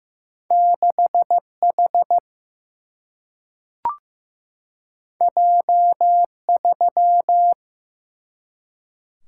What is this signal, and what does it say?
Morse code, heard as a keyed tone, about 15 wpm, 700 Hz. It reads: 6 H J 3
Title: Single-Letter Single-Number 15wpm